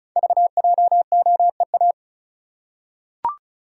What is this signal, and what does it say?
Morse code, heard as a keyed tone, about 36 wpm, 700 Hz. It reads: VJOEA
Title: Code Groups - 5 Letters - 35wpm